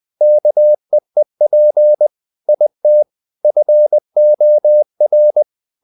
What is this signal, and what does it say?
Morse code, heard as a keyed tone, about 20 wpm, 600 Hz. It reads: KEEP IT FOR
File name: the old man and the sea part 2 20wpm